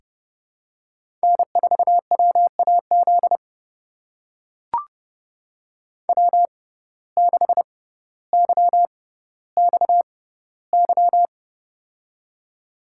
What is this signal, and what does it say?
Morse code, heard as a keyed tone, about 31 wpm, spaced as for 16 wpm, 700 Hz. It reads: N4WAZ W6YXY
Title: USA Call Signs - 6x Character Spacing - 30wpm